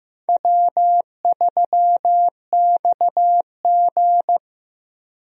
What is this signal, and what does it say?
Morse code, heard as a keyed tone, about 15 wpm, 700 Hz. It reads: W3XG
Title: USA Call Signs - 4 Characters - Repeated 3x - 15wpm